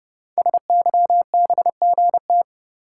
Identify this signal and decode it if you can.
Morse code, heard as a keyed tone, about 28 wpm, 700 Hz. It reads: SYBGT
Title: Code Groups - 5 Letters - 30wpm